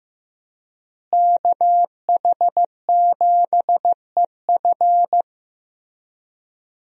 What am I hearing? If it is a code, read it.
Morse code, heard as a keyed tone, about 15 wpm, 700 Hz. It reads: KH7EF
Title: USA Call Signs - 5 Characters - Repeated 2x - 15wpm